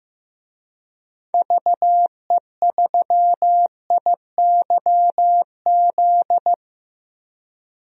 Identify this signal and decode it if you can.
Morse code, heard as a keyed tone, about 15 wpm, 700 Hz. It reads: VE3IYZ
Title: DXCC-100 Call Signs - 6 Characters - Repeated 3x - 15wpm